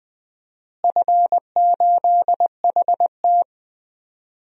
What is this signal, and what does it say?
Morse code, heard as a keyed tone, about 20 wpm, 700 Hz. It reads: F8HT
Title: DXCC-100 Call Signs - 4 Characters - Repeated 2x - 20wpm